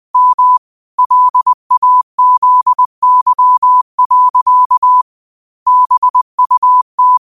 Morse code, heard as a keyed tone, about 20 wpm, 1000 Hz. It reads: M LAZY. BUT
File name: qotdfiles / 20wpm